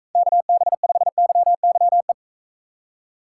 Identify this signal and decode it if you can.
Morse code, heard as a keyed tone, about 42 wpm, 700 Hz. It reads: KB5YYE